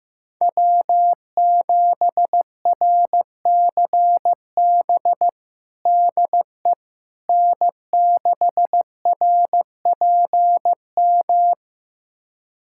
Morse code, heard as a keyed tone, about 15 wpm, 700 Hz. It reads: W7RCB DE N6RPM
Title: USA Callsign DE Callsign - 5 Characters - 15wpm